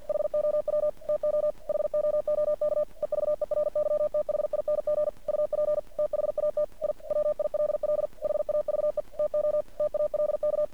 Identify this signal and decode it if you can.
Morse code, heard as a keyed tone, about 50 wpm, 600 Hz. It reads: HOW TO HOOK EVERYTHING UP THAT I WILL HAVE TO TALK
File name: archived / CRSnet